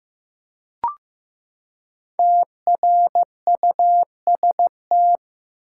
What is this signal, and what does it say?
Morse code, heard as a keyed tone, about 15 wpm, 700 Hz. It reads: TRUST
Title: Five Letter Words - 15wpm